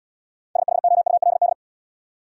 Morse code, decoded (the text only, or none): SHOULD